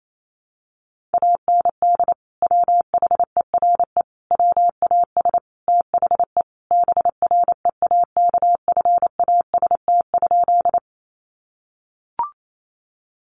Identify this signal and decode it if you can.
Morse code, heard as a keyed tone, about 28 wpm, 700 Hz. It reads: AND WHERE WAS THE BREAKFAST?
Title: Sentences from Top 1500 Words 28wpm